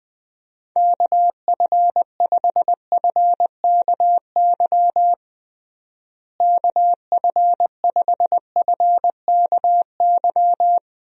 Morse code, heard as a keyed tone, about 21 wpm, 700 Hz. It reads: KF5FKY KF5FKY
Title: USA Call Signs - 6 Characters - Repeated 2x - 20wpm